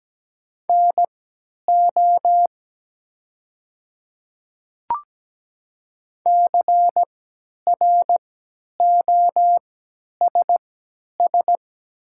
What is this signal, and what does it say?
Morse code, heard as a keyed tone, about 17 wpm, spaced as for 10 wpm, 700 Hz. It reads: NO CROSS
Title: 3x Character Spacing - Top 300 Words - 17wpm